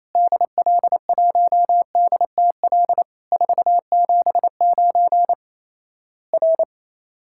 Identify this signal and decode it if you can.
Morse code, heard as a keyed tone, about 28 wpm, 700 Hz. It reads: DL1DTL 479 R
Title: Worked All Europe DX Contest - QTC Exchange - 28wpm